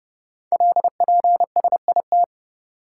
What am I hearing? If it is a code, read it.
Morse code, heard as a keyed tone, about 29 wpm, 700 Hz. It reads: LPSIT
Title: Code Groups - 5 Letters - 30wpm